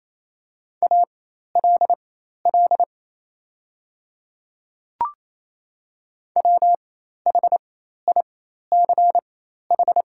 Morse code, heard as a keyed tone, about 27 wpm, spaced as for 13 wpm, 700 Hz. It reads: ALL WHICH
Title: 4x Character Spacing - Top 200 Words - 28wpm